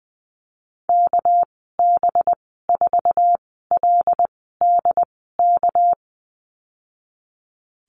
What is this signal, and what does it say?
Morse code, heard as a keyed tone, about 20 wpm, spaced as for 15 wpm, 700 Hz. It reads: KB4LDK
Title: 207 Lesson 33g - Letter j - Call Signs 20wpm